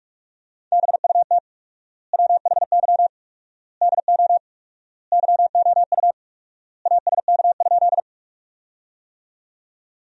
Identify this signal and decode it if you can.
Morse code, heard as a keyed tone, about 41 wpm, 700 Hz. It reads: BUT WHY DO YOU ASK?
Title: 4x Word-Spacing - Sentences from Top 200 Words - 45wpm